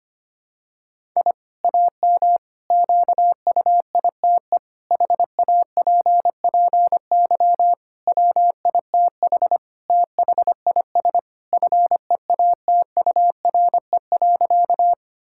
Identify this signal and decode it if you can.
Morse code, heard as a keyed tone, about 25 wpm, 700 Hz. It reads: I AM QUITE HAPPY WITH THIS FEATURE.
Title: Sentences from Top 1000 Words - Review Part 5 - 25wpm